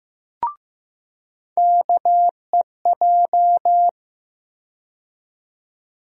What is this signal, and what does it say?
Morse code, heard as a keyed tone, about 15 wpm, 700 Hz. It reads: KEJ
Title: USA Call Sign Postfixes - 3 Letters 15wpm